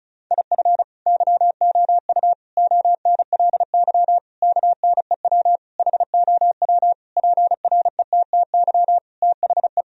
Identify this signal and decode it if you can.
Morse code, heard as a keyed tone, about 36 wpm, 700 Hz. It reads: IF YOU ONLY KNEW HOW PRETTY THE
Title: Sentences from Top 1000 Words - Review Part 4 - 35wpm